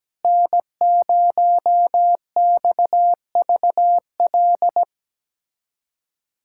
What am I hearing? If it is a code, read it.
Morse code, heard as a keyed tone, about 17 wpm, 700 Hz. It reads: N0XVL
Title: USA Callsign DE Callsign 17wpm